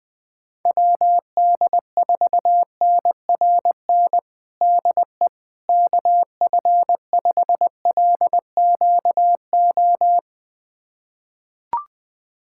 Morse code, heard as a keyed tone, about 20 wpm, 700 Hz. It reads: WD4NRN DE KF5LQO
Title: USA Callsign DE Callsign - 6 Characters - 20wpm